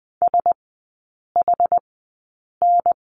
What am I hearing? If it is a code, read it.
Morse code, heard as a keyed tone, about 20 wpm, spaced as for 8 wpm, 690 Hz. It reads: SHN
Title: 145 Lesson 24e - Letter y - Three Characters 20wpm